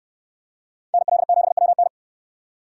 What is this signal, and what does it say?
Morse code, heard as a keyed tone, about 69 wpm, 690 Hz. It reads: NL7WN